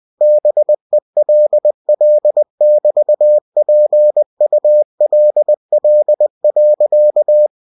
Morse code, heard as a keyed tone, about 20 wpm, 600 Hz. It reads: BELL<BT>PULL.
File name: cw-books / winnie-the-pooh / winnie-20wpm-page-0028